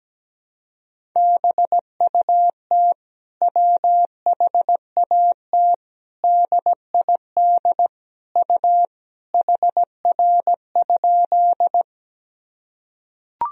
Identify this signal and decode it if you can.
Morse code, heard as a keyed tone, about 17 wpm, 700 Hz. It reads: BUT WHAT DID U HR?
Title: Sentences from Top 300 Words and Abbreviations 17wpm